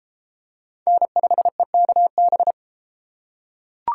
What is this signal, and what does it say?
Morse code, heard as a keyed tone, about 31 wpm, 700 Hz. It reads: N5EKB